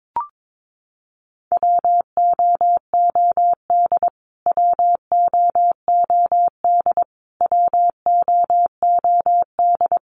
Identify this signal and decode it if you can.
Morse code, heard as a keyed tone, about 22 wpm, 700 Hz. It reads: WOOD WOOD WOOD
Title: Top 500 Words Repeated 22wpm